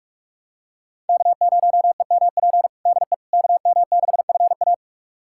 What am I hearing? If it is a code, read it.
Morse code, heard as a keyed tone, about 46 wpm, 700 Hz. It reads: K0EMP DE KM6FA
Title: USA Callsign DE Callsign - 5 Characters - 45wpm